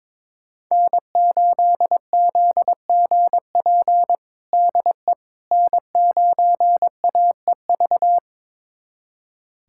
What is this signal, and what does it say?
Morse code, heard as a keyed tone, about 22 wpm, 700 Hz. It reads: N8ZGP DE N9AEV
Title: USA Callsign DE Callsign - 5 Characters - 22wpm